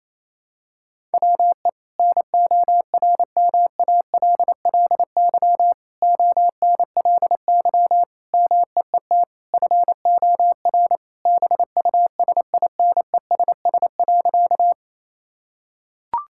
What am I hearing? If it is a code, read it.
Morse code, heard as a keyed tone, about 28 wpm, 700 Hz. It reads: WE NORMALLY ONLY MEET FOR BUSINESS.
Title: Sentences from Top 1000 Words - Review Part 6 - 28wpm